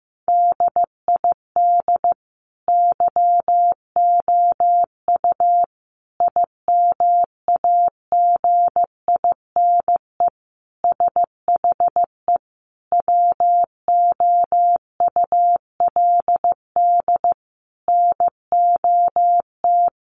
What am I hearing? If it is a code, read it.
Morse code, heard as a keyed tone, about 15 wpm, 700 Hz. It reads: DID YOU IMAGINE SHE WOULD NOT
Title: Sentences from Top 1400 Words 15wpm